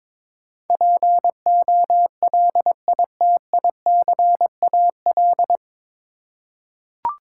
Morse code, heard as a keyed tone, about 21 wpm, 700 Hz. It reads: POLITICAL
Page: Morse Code Ninja practice 22wpm